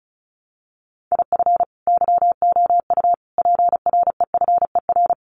Morse code, heard as a keyed tone, about 34 wpm, 700 Hz. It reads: IF YOU PREFER
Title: Sentences from Top 1100 Words 35wpm